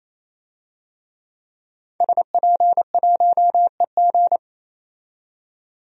Morse code, heard as a keyed tone, about 28 wpm, 700 Hz. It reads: SP1EG
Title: DXCC-100 Call Signs - 5 Characters - 28wpm